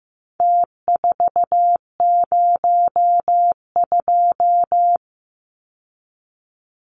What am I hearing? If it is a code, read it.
Morse code, heard as a keyed tone, about 15 wpm, 700 Hz. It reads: T402
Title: Numbers - Contest Style - 4-Digit 15wpm